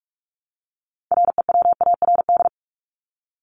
Morse code, heard as a keyed tone, about 44 wpm, 700 Hz. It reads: REWARD
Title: Top 1751-2000 Words 45wpm